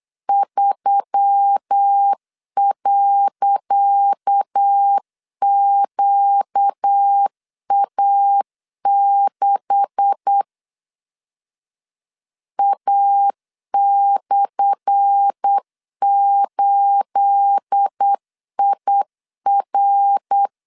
Morse code, heard as a keyed tone, about 8 wpm, 800 Hz. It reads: 3.QA6 A/8IR
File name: LEZIONI / 10 SIGNS / 10C DOT 8wpm 3649